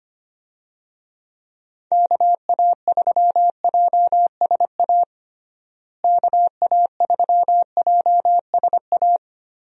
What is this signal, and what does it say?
Morse code, heard as a keyed tone, about 25 wpm, 700 Hz. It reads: KA3JSA KA3JSA
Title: USA Call Signs - 6 Characters - Repeated 3x - 25wpm